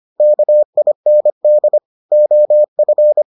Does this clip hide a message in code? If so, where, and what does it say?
Yes: Morse code from 0.2 to 3.2 s, heard as a keyed tone, about 25 wpm, 600 Hz. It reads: KIND OF